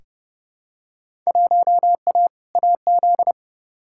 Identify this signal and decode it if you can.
Morse code, heard as a keyed tone, about 31 wpm, 700 Hz. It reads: 1A AZ
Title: Field Day Exchanges - 30wpm